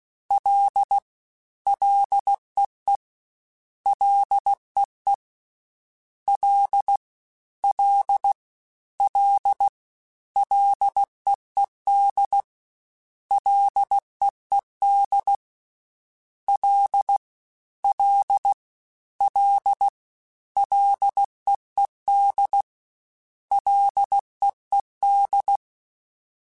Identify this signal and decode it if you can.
Morse code, heard as a keyed tone, about 16 wpm, 800 Hz. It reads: L LEE LEE L L L LEED LEED L L L LEED LEED
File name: Ls / Single L m3s / Single L @15